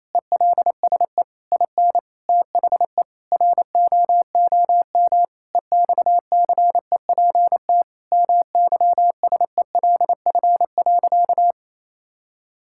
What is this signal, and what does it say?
Morse code, heard as a keyed tone, about 27 wpm, 700 Hz. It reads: ELSE IN THE ROOM EXCEPT MYSELF.
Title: Sentences from Top 1600 Words 28wpm